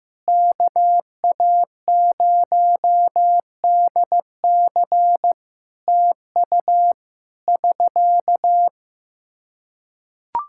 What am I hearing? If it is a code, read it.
Morse code, heard as a keyed tone, about 16 wpm, 700 Hz. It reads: KA0DC TU <SK>
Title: Practice QSO - Exchange 4 - Ending - 15wpm